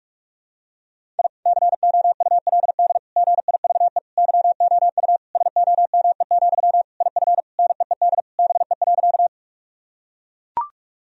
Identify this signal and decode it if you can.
Morse code, heard as a keyed tone, about 41 wpm, 700 Hz. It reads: I COULD GIVE YOU SOME, IF NEED BE.
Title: Sentences from Top 1000 Words - Review Part 1 - 45wpm